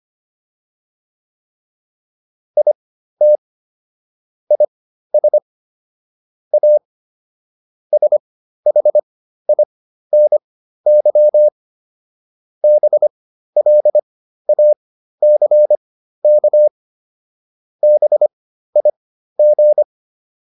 Morse code, heard as a keyed tone, about 25 wpm, spaced as for 13 wpm, 600 Hz. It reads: IT IS A SHINY BLACK BIG